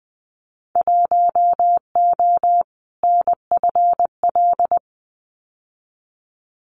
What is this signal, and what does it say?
Morse code, heard as a keyed tone, about 20 wpm, 700 Hz. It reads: 1O NFL
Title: Winter Field Day Exchanges 20wpm